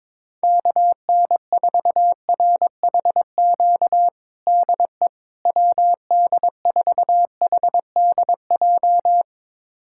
Morse code, heard as a keyed tone, about 22 wpm, 700 Hz. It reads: KN4RHQ DE WD4HDJ
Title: USA Callsign DE Callsign - 6 Characters - 22wpm